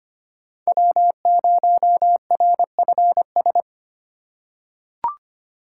Morse code, heard as a keyed tone, about 25 wpm, 700 Hz. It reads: W0RFS
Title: USA Call Signs - 5 Characters - 3x Character Spacing - 25wpm